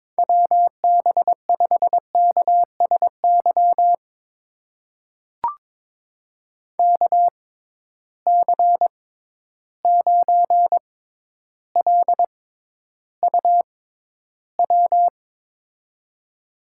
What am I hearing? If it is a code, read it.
Morse code, heard as a keyed tone, about 22 wpm, spaced as for 11 wpm, 700 Hz. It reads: WB5KSY KC9LUW
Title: USA Call Signs - 6 Characters - 6x Character Spacing - 22wpm